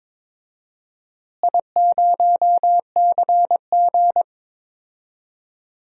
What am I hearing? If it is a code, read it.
Morse code, heard as a keyed tone, about 22 wpm, 700 Hz. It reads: I0CG
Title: DXCC-100 Call Signs - 4 Characters - Repeated 2x - 22wpm